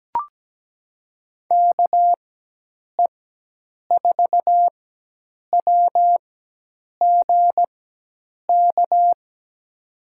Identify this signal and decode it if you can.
Morse code, heard as a keyed tone, about 17 wpm, spaced as for 8 wpm, 700 Hz. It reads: KE4WGK
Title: USA Call Signs - 6 Characters - 4x Character Spacing - 17wpm